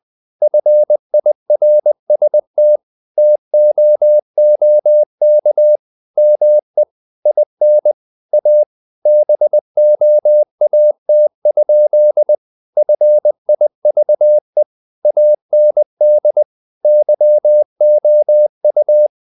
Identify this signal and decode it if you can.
Morse code, heard as a keyed tone, about 20 wpm, 600 Hz. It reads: FIRST TOOK ME IN A BOAT? FIVE AND YOU
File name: the old man and the sea / the old man and the sea part 1 20wpm